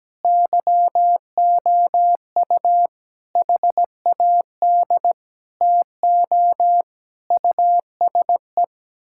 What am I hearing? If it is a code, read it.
Morse code, heard as a keyed tone, about 17 wpm, 700 Hz. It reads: YOU HAD TO USE
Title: Sentences from Top 1000 Words - Review Part 6 - 17wpm